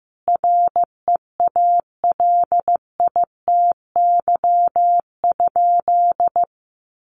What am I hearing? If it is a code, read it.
Morse code, heard as a keyed tone, about 15 wpm, 700 Hz. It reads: REALITY?
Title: Sentences from Top 1200 Words 15wpm